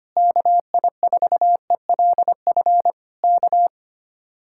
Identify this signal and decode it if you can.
Morse code, heard as a keyed tone, about 25 wpm, 700 Hz. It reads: KI4ELF K
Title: Speed-Racing Common QSO Phrases - 25wpm